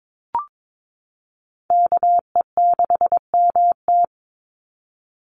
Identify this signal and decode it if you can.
Morse code, heard as a keyed tone, about 22 wpm, 700 Hz. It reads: KE6MT
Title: NA SOTA Chaser Callsigns 22wpm